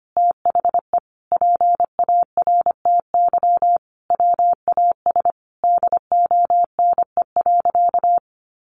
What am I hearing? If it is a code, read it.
Morse code, heard as a keyed tone, about 25 wpm, 700 Hz. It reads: THE PARTY WAS DONE.
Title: Sentences from Top 1400 Words 25wpm